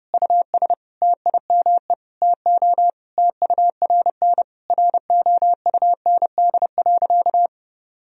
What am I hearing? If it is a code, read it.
Morse code, heard as a keyed tone, about 32 wpm, 700 Hz. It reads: US TIME TO TURN ROUND.
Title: Sentences from Top 1000 Words - Review Part 1 - 30wpm